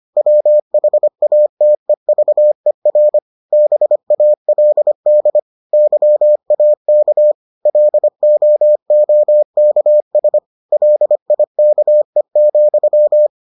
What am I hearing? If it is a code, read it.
Morse code, heard as a keyed tone, about 25 wpm, 600 Hz. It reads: WHATEVER BALD YAK LOOKS LIKE,